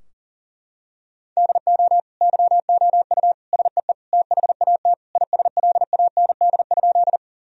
Morse code, heard as a keyed tone, about 37 wpm, 700 Hz. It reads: DO YOU SEE THAT ISLAND?
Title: Sentences from Top 1000 Words - Review Part 3 - 40wpm